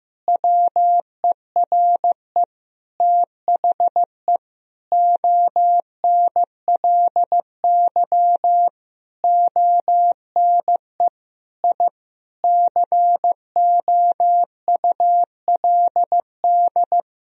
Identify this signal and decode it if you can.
Morse code, heard as a keyed tone, about 16 wpm, 700 Hz. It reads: WERE THE ONLY ONE I COULD
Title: Sentences from Top 1000 Words - Review Part 4 - 15wpm